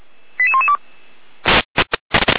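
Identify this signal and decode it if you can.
Five-tone selective call (ZVEI), heard as quick runs of five tones, tones of about 70 ms. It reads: 88112
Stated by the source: ZVEI five-tone selective call 88118, received on SDR.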